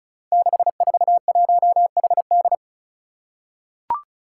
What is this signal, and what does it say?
Morse code, heard as a keyed tone, about 36 wpm, 700 Hz. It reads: 641HD
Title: Five Challenging Letter-Number Combinations 35wpm